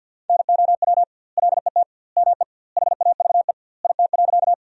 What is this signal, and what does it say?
Morse code, heard as a keyed tone, about 50 wpm, 700 Hz. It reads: NOW LET ME HAVE IT.
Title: 2x Word-Spacing - Sentences from Top 300 Words - 50wpm